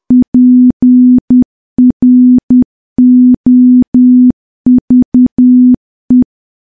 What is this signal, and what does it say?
Morse code, heard as a keyed tone, about 10 wpm, 250 Hz. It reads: PROVE
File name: morse / morse-training-10WPM